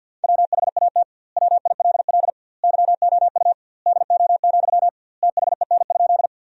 Morse code, heard as a keyed tone, about 50 wpm, 700 Hz. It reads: WHAT WILL YOU DO, THEN?
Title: 2x Word-Spacing - Sentences from Top 100 Words - 50wpm